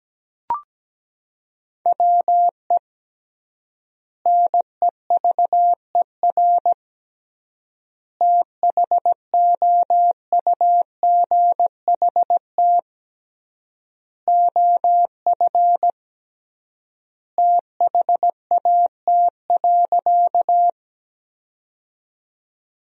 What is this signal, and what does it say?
Morse code, heard as a keyed tone, about 17 wpm, 700 Hz. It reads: WE NEVER THOUGHT OF THAT.
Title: 3x Word-Spacing - Sentences from Top 300 Words - 17wpm